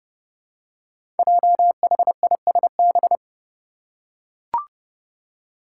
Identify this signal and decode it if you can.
Morse code, heard as a keyed tone, about 30 wpm, 700 Hz. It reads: JHISB
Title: Five Challenging Letter-Number Combinations 30wpm